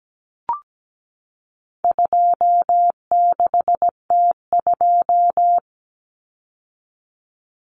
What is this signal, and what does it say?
Morse code, heard as a keyed tone, about 17 wpm, 700 Hz. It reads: 26T2
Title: Common Cut Numbers - 4-Digit 17wpm